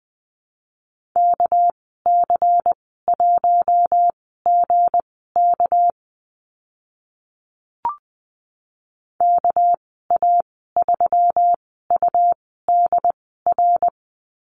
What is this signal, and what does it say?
Morse code, heard as a keyed tone, about 20 wpm, spaced as for 14 wpm, 700 Hz. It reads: KC1GK KA3UDR
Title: 219 Lesson 35g - Number 0 - Call Signs 20wpm